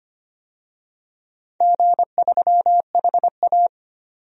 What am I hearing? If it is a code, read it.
Morse code, heard as a keyed tone, about 25 wpm, 700 Hz. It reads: G3HA
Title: DXCC-100 Call Signs - 4 Characters - 25wpm